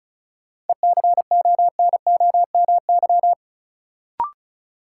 Morse code, heard as a keyed tone, about 36 wpm, 700 Hz. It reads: ECONOMY
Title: Sentences from Top 1000 Words - Review Part 3 - New Word Drill - 35wpm